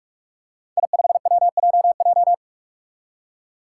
Morse code, heard as a keyed tone, about 46 wpm, 700 Hz. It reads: I5WJJ
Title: Five Challenging Letter-Number Combinations 45wpm